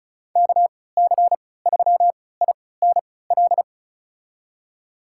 Morse code, heard as a keyed tone, about 36 wpm, spaced as for 20 wpm, 700 Hz. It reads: KC3INL
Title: USA Call Signs - 6 Characters - 3x Character Spacing - 35wpm